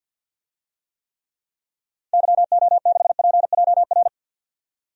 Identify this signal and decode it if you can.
Morse code, heard as a keyed tone, about 51 wpm, 700 Hz. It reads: YO6PJR
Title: DXCC-100 Call Signs - 6 Characters - 50wpm